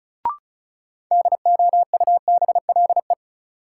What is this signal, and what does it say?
Morse code, heard as a keyed tone, about 36 wpm, 700 Hz. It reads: DOUBLE
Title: Sentences from Top 1400 Words - New Word Drill - 35wpm